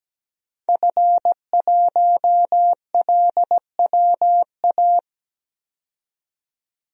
Morse code, heard as a keyed tone, about 17 wpm, 700 Hz. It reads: F1LWA
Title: DXCC-100 Call Signs - 5 Characters - 17wpm